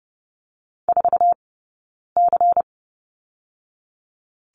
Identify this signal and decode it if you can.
Morse code, heard as a keyed tone, about 31 wpm, 700 Hz. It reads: 4 C
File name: Lesson 16d - Letter c - Two Characters 30wpm